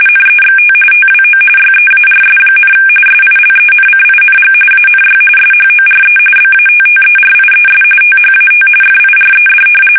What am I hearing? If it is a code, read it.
RTTY, about 45.45 baud. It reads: NEXT TWO YEARS, SILENCING ANY SHORE BATTERIES THAT FIRED
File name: fsk850 / battleshipnj